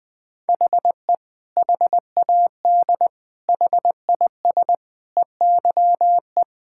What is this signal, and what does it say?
Morse code, heard as a keyed tone, about 20 wpm, 700 Hz. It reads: HE HAD HIS EYE
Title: Sentences from Top 1000 Words - Review Part 5 - 20wpm